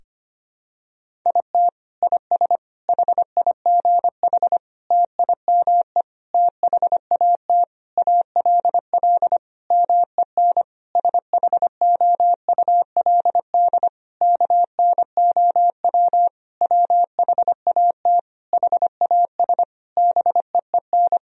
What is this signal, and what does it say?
Morse code, heard as a keyed tone, about 25 wpm, 700 Hz. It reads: IT IS HIGH TIME THAT ALL MEN SHOULD KNOW WHAT HAS BEEN
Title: Sentences from Top 1000 Words - Review Part 1 - 25wpm